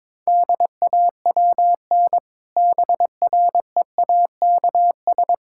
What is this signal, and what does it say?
Morse code, heard as a keyed tone, about 22 wpm, 700 Hz. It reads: DAWN BREAKS
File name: Difficult Sentences with Hints - 22wpm